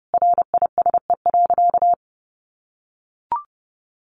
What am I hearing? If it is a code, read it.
Morse code, heard as a keyed tone, about 28 wpm, 700 Hz. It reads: RISE.
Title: Sentences from Top 1300 Words 30wpm